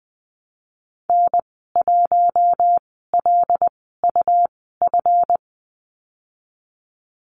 Morse code, heard as a keyed tone, about 20 wpm, 700 Hz. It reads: N 1 L U F
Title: 154 Lesson 25g - Letter p - Call Signs 20wpm